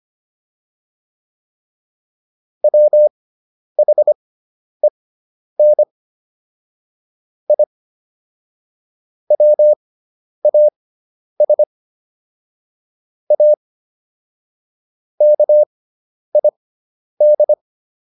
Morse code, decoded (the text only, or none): WHEN I WAS A KID